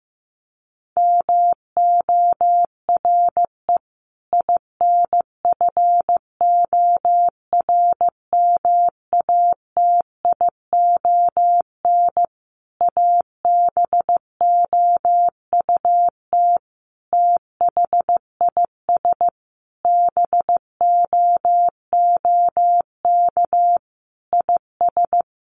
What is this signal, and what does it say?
Morse code, heard as a keyed tone, about 15 wpm, 700 Hz. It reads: MORE INFORMATION ABOUT THIS BOOK IS
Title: Sentences from Top 1200 Words 15wpm